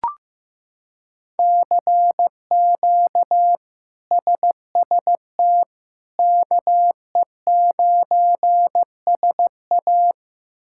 Morse code, heard as a keyed tone, about 15 wpm, 700 Hz. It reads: CQ SST KE9SA
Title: K1USN SST QSO Practice - 15wpm